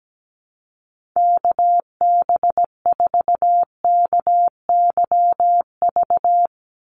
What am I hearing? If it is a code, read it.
Morse code, heard as a keyed tone, about 17 wpm, 700 Hz. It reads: KB4KYV
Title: USA Call Signs 17wpm